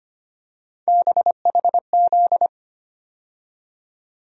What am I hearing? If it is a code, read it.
Morse code, heard as a keyed tone, about 25 wpm, 700 Hz. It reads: BHZ
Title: USA Call Sign Postfixes - 3 Letters 25wpm